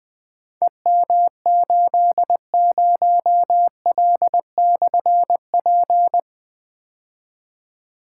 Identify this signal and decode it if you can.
Morse code, heard as a keyed tone, about 21 wpm, 700 Hz. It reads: EM80L/P
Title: Difficult Call Signs 20wpm